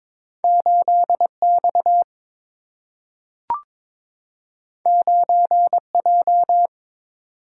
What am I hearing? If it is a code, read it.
Morse code, heard as a keyed tone, about 22 wpm, 700 Hz. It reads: 8 X 9 J
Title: Single-Letter Single-Number 22wpm